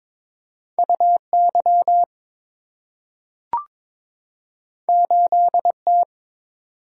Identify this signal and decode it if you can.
Morse code, heard as a keyed tone, about 22 wpm, 690 Hz. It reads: UY 8T
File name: practice / 22wpm / Two Letter-Number Combinations - 22wpm